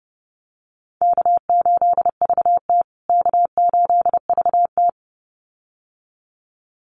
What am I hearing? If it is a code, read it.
Morse code, heard as a keyed tone, about 30 wpm, 700 Hz. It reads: K8VT K8VT
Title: USA Call Signs Repeated 2x 30wpm